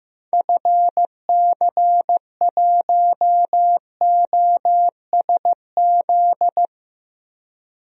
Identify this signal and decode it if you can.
Morse code, heard as a keyed tone, about 15 wpm, 700 Hz. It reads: FC1OSZ
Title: DXCC-100 Call Signs - 6 Characters - Repeated 3x - 15wpm